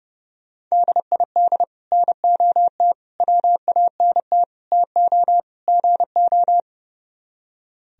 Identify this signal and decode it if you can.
Morse code, heard as a keyed tone, about 38 wpm, 700 Hz. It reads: DID NOT WANT TO GO
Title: Speed-Racing - Sets of 5 Words 30wpm